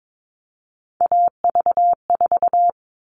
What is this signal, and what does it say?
Morse code, heard as a keyed tone, about 22 wpm, 700 Hz. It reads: AV4
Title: All Cut Numbers - 3-Digit 22wpm